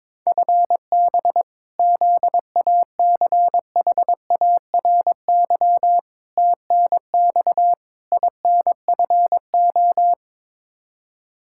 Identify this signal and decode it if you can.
Morse code, heard as a keyed tone, about 22 wpm, 700 Hz. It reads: FB ZACHARY TNX INFO
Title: Practice QSO - Exchange 3 - More Info - 22wpm